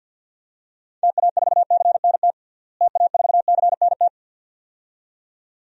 Morse code, heard as a keyed tone, about 51 wpm, 700 Hz. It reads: TA4CNT TA4CNT
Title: DXCC-100 Call Signs - 6 Characters - Repeated 2x - 50wpm